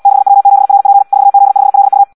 Morse code, heard as a keyed tone, about 74 wpm, 800 Hz. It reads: 6W7RP 4Z5LU
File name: Audio practica CW / qrq-tx-78 2